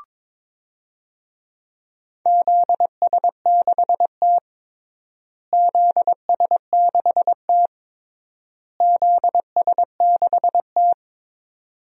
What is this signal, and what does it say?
Morse code, heard as a keyed tone, about 23 wpm, 700 Hz. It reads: ZS6T ZS6T ZS6T
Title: DXCC-100 Call Signs - 4 Characters - Repeated 3x - 22wpm